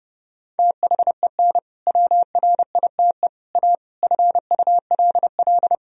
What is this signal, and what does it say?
Morse code, heard as a keyed tone, about 30 wpm, 700 Hz. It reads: THEN WRITE A FULL
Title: Sentences from Top 1000 Words - Review Part 3 - 30wpm